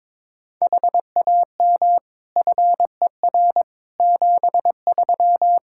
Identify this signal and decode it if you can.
Morse code, heard as a keyed tone, about 21 wpm, 700 Hz. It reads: HAM FER 73
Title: Practice QSO - Exchange 3 - More Info - 22wpm